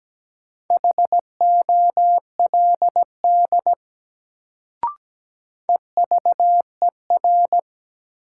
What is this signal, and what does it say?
Morse code, heard as a keyed tone, about 17 wpm, 700 Hz. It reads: HOLD EVER
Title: Sentences from Top 400 Words - New Word Drill - 17wpm